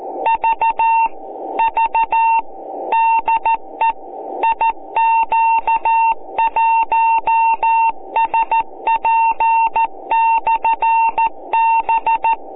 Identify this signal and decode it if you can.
Morse code, heard as a keyed tone, about 14 wpm, 910 Hz. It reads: V V DE IQ1SP/B